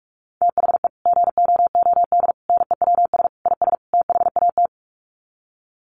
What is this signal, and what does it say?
Morse code, heard as a keyed tone, about 44 wpm, 700 Hz. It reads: THE GOOD NEWS IS THAT
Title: Sets of 5 Words 45wpm